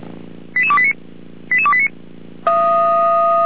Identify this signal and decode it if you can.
Five-tone selective call (ZVEI), heard as quick runs of five tones, tones of about 70 ms. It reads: 88289 88289